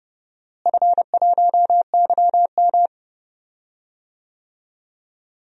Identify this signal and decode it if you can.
Morse code, heard as a keyed tone, about 30 wpm, 700 Hz. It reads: F1YM
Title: DXCC-100 Call Signs - 4 Characters - Repeated 2x - 30wpm